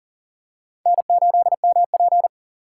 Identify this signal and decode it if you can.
Morse code, heard as a keyed tone, about 40 wpm, 700 Hz. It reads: N8MP
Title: USA Call Signs - 4 Characters - 6x Character Spacing - 40wpm